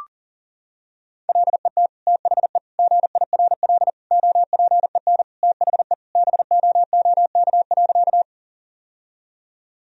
Morse code, heard as a keyed tone, about 40 wpm, 700 Hz. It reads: LET THE GIRL OPEN THE BOOK.
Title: Sentences from Top 1000 Words - Review Part 1 - 40wpm